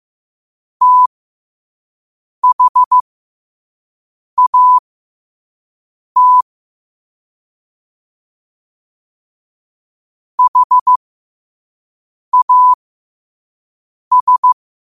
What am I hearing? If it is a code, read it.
Morse code, heard as a keyed tone, about 15 wpm, spaced as for 5 wpm, 1000 Hz. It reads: THAT HAS
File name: qotdfiles / 5wpm